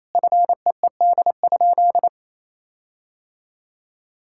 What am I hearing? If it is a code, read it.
Morse code, heard as a keyed tone, about 28 wpm, 700 Hz. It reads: FEED?
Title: Sentences from Top 1600 Words 28wpm